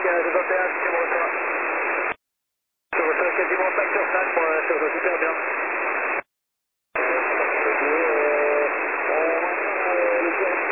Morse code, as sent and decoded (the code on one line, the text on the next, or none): none